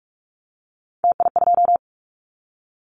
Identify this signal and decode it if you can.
Morse code, heard as a keyed tone, about 44 wpm, 700 Hz. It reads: TI2
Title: Callsign Prefixes for Top 100 DXCC - 45wpm